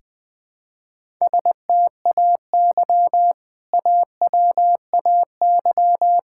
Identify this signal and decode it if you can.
Morse code, heard as a keyed tone, about 20 wpm, 700 Hz. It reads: STAY AWAY
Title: Sentences with 2-Word Phrases - 20wpm